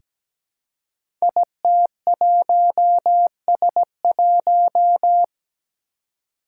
Morse code, heard as a keyed tone, about 17 wpm, 700 Hz. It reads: IT1S1
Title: Five Challenging Letter-Number Combinations 17wpm